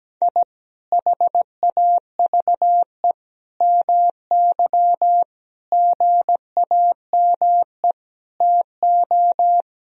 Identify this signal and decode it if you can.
Morse code, heard as a keyed tone, about 17 wpm, 700 Hz. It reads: I HAVE MY GAME TO
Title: Sentences from Top 1000 Words - Review Part 3 - 17wpm